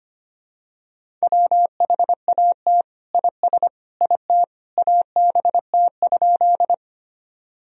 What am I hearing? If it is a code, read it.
Morse code, heard as a keyed tone, about 25 wpm, 700 Hz. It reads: WHAT IS IT ABT?
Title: Sentences from Top 100 Words and Abbreviations 25wpm